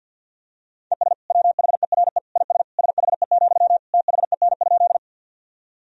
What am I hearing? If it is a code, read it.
Morse code, heard as a keyed tone, about 51 wpm, 700 Hz. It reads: ES WHERE IS SHE, THEN?
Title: Sentences from Top 200 Words and Abbreviations 50wpm